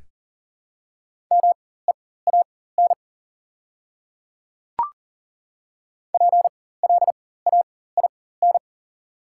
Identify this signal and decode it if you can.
Morse code, heard as a keyed tone, about 37 wpm, spaced as for 18 wpm, 690 Hz. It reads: MEAN PLAIN
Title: 4x Character Spacing - Top 500 Words - 40wpm